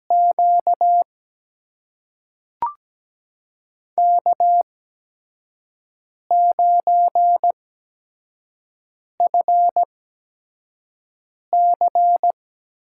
Morse code, heard as a keyed tone, about 17 wpm, spaced as for 5 wpm, 700 Hz. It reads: Q K9FC
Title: USA Call Signs - 8x Character Spacing - 17wpm